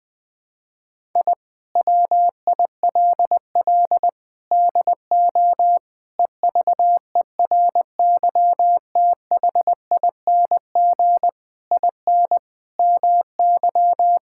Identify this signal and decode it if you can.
Morse code, heard as a keyed tone, about 21 wpm, 700 Hz. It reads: I WILL DO EVERYTHING IN MY
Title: Sentences with 8-Word Phrases - Spoken Beginning - 20wpm